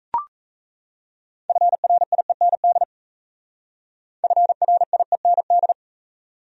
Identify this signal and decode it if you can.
Morse code, heard as a keyed tone, about 35 wpm, 700 Hz. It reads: FRIEND FRIEND
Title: Speed-Racing Top 300 Words 28wpm